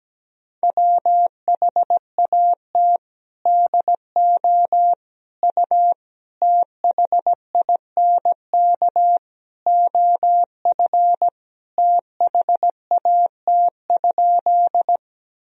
Morse code, heard as a keyed tone, about 17 wpm, 700 Hz. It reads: WHAT DO U THINK OF THAT?
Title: Sentences from Top 200 Words and Abbreviations 17wpm